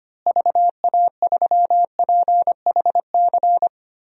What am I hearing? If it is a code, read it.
Morse code, heard as a keyed tone, about 25 wpm, 700 Hz. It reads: VA3PHC
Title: DXCC-100 Call Signs - 6 Characters - Repeated 3x - 25wpm